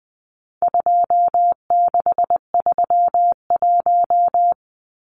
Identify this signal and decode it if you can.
Morse code, heard as a keyed tone, about 20 wpm, 700 Hz. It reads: 2631
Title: Numbers - 4-Digit 20wpm